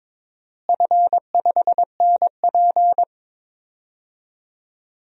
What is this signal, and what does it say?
Morse code, heard as a keyed tone, about 22 wpm, 700 Hz. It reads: F5NP
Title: DXCC-100 Call Signs - 4 Characters - Repeated 3x - 22wpm